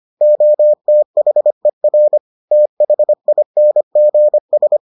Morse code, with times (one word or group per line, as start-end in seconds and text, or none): 0.2-2.2 OTHER
2.5-4.8 THINGS